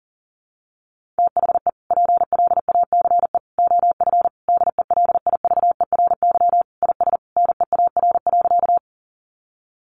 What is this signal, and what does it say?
Morse code, heard as a keyed tone, about 40 wpm, 700 Hz. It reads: THE PLACE OF DELIVERY IS NEAR.
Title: Sentences from Top 1500 Words 40wpm